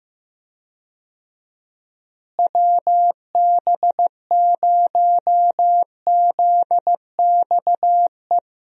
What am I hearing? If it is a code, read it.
Morse code, heard as a keyed tone, about 15 wpm, 700 Hz. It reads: WB0ZXE
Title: USA Call Signs - 6 Characters - Repeated 3x - 15wpm